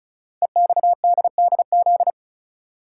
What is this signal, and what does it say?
Morse code, heard as a keyed tone, about 35 wpm, 690 Hz. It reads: EXDDZ
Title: Code Groups - 5 Letters - 35wpm